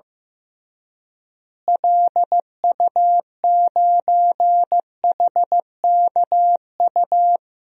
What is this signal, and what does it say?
Morse code, heard as a keyed tone, about 15 wpm, 700 Hz. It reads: LU9HKU
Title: DXCC-100 Call Signs - 6 Characters - Repeated 3x - 15wpm